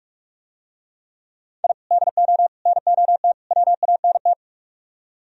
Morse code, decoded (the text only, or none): I DO NOT WANT